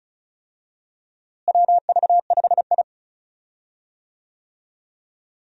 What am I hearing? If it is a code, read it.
Morse code, heard as a keyed tone, about 36 wpm, 700 Hz. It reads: WV5I
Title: USA Call Signs - 4 Characters - Repeated 2x - 35wpm